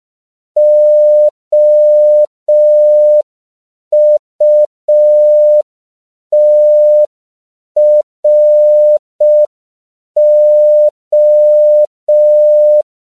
Morse code, heard as a keyed tone, about 5 wpm, 600 Hz. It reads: OUTRO
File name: 5PPM